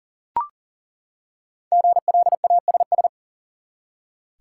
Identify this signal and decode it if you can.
Morse code, heard as a keyed tone, about 40 wpm, 700 Hz. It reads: GLASS
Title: Five Letter Words - 40wpm